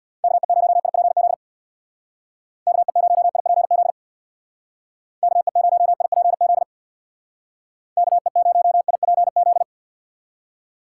Morse code, heard as a keyed tone, about 63 wpm, 700 Hz. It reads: KE0IPZ KE0IPZ KE0IPZ KE0IPZ